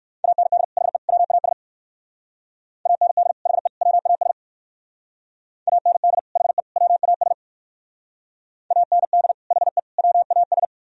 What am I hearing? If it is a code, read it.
Morse code, heard as a keyed tone, about 58 wpm, 700 Hz. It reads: AND HE WAS AND HE WAS AND HE WAS AND HE WAS